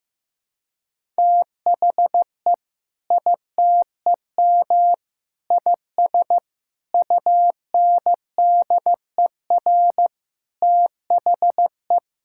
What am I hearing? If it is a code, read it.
Morse code, heard as a keyed tone, about 15 wpm, 700 Hz. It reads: THE ITEM IS UNDER THE
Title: Sentences from Top 1000 Words - Review Part 6 - 15wpm